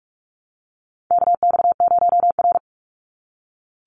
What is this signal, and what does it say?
Morse code, heard as a keyed tone, about 46 wpm, 700 Hz. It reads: KX0R
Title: NA SOTA Chaser Callsigns 45wpm